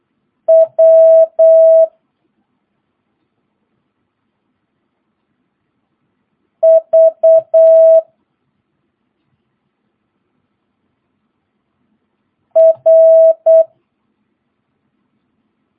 Morse code, heard as a keyed tone, about 8 wpm, 650 Hz. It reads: W V R
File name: jh8cbh / 2022cw